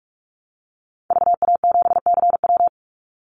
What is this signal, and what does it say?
Morse code, heard as a keyed tone, about 44 wpm, 700 Hz. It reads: VA7CW
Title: Call Signs Repeated 3x - 45wpm